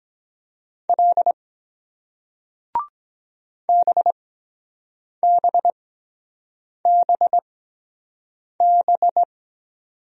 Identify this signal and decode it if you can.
Morse code, heard as a keyed tone, about 23 wpm, spaced as for 7 wpm, 700 Hz. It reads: L BBBB